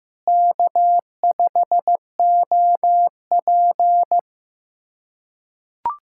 Morse code, heard as a keyed tone, about 15 wpm, 700 Hz. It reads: K5OP
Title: USA Callsign DE Callsign - 4 Characters - 15wpm